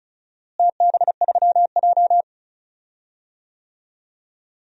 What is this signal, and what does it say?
Morse code, heard as a keyed tone, about 34 wpm, 700 Hz. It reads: TB3J
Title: DXCC-100 Call Signs - 4 Characters - Repeated 2x - 35wpm